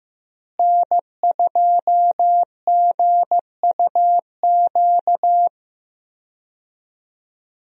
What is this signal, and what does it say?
Morse code, heard as a keyed tone, about 15 wpm, 700 Hz. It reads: N2GUQ
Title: USA Call Signs - 5 Characters - Repeated 2x - 15wpm